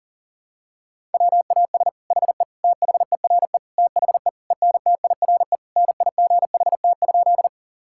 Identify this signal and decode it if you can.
Morse code, heard as a keyed tone, about 37 wpm, 700 Hz. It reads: WAS HE THERE THE ENTIRE NIGHT?
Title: Sentences from Top 1000 Words - Review Part 5 - 40wpm